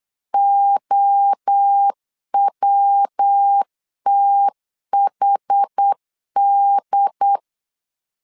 Morse code, heard as a keyed tone, about 8 wpm, 800 Hz. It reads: OWTHD
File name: CW / DET / 1 3 1500C S B 8wpm 4141